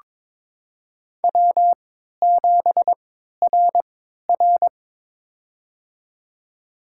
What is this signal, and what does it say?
Morse code, heard as a keyed tone, about 22 wpm, spaced as for 13 wpm, 700 Hz. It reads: W7RR
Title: USA Call Signs - 4 Characters - 3x Character Spacing - 22wpm